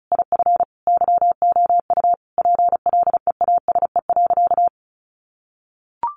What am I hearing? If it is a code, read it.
Morse code, heard as a keyed tone, about 34 wpm, 700 Hz. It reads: IF YOU PLEASE.
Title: Sentences from Top 1500 Words 35wpm